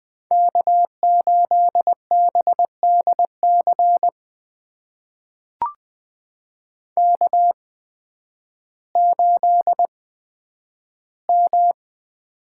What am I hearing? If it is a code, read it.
Morse code, heard as a keyed tone, about 20 wpm, spaced as for 11 wpm, 700 Hz. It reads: K8BDC K8M
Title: USA Call Signs - 5 Characters - 8x Character Spacing - 20wpm